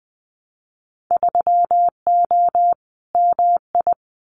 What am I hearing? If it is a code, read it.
Morse code, heard as a keyed tone, about 19 wpm, 700 Hz. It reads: 3O MI
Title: Winter Field Day Exchanges 20wpm